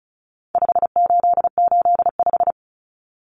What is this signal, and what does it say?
Morse code, heard as a keyed tone, about 34 wpm, 700 Hz. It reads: 5885
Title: Numbers - 4-Digit 35wpm